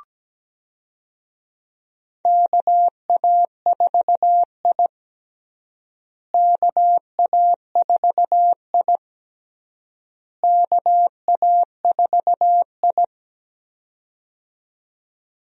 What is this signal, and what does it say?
Morse code, heard as a keyed tone, about 17 wpm, 700 Hz. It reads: KA4I KA4I KA4I
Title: USA Call Signs - 4 Characters - Repeated 3x - 17wpm